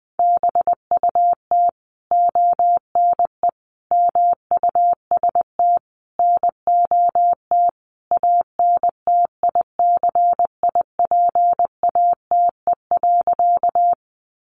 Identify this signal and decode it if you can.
Morse code, heard as a keyed tone, about 20 wpm, 700 Hz. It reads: BUT ONE MUST NOT ANTICIPATE.
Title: Sentences from Top 1500 Words 20wpm